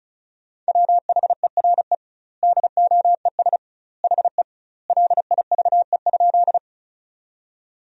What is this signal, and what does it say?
Morse code, heard as a keyed tone, about 36 wpm, 700 Hz. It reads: WHERE DOES HE LIVE?
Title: 2x Word-Spacing - Sentences from Top 200 Words - 35wpm